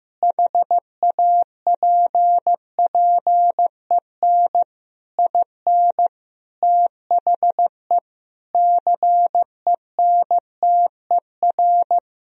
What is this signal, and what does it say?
Morse code, heard as a keyed tone, about 15 wpm, 700 Hz. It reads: HAPPEN IN THE CENTER
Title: Sentences from Top 1000 Words - Review Part 2 - 15wpm